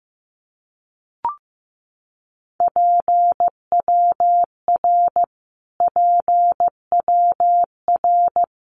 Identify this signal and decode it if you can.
Morse code, heard as a keyed tone, about 15 wpm, 700 Hz. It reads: PWR PWR
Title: Abbreviations Repeated 3x - 15wpm